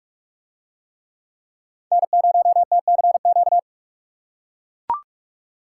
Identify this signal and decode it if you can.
Morse code, heard as a keyed tone, about 42 wpm, 700 Hz. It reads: N0TCQ